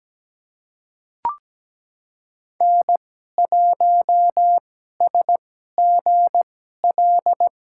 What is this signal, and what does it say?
Morse code, heard as a keyed tone, about 17 wpm, spaced as for 12 wpm, 700 Hz. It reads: N1SGL
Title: USA Call Signs - 2x Character Spacing - 17wpm